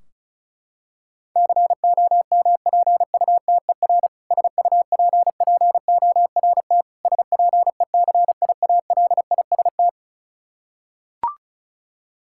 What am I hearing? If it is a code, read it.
Morse code, heard as a keyed tone, about 36 wpm, 700 Hz. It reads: COMPUTER SUPPORT SPECIALIST
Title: K1USN SST QSO Practice 35wpm